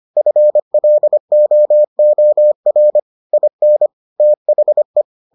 Morse code, heard as a keyed tone, about 25 wpm, 600 Hz. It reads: FLOOR IN THE